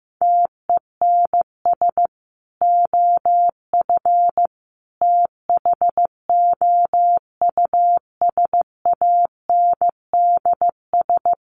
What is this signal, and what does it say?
Morse code, heard as a keyed tone, about 15 wpm, 700 Hz. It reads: TENS OF THOUSANDS
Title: Sets of 5 Words Encore2 15wpm